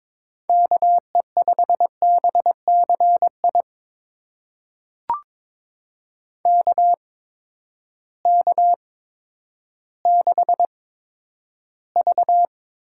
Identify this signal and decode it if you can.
Morse code, heard as a keyed tone, about 21 wpm, spaced as for 11 wpm, 700 Hz. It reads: KE5BCI KK6V
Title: USA Call Signs - 6 Characters - 8x Character Spacing - 22wpm